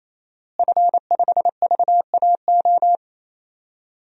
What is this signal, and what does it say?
Morse code, heard as a keyed tone, about 28 wpm, 700 Hz. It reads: F5VAO
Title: DXCC-100 Call Signs - 5 Characters - Repeated 3x - 28wpm